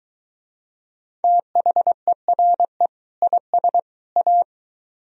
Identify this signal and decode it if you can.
Morse code, heard as a keyed tone, about 22 wpm, 700 Hz. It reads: THERE IS A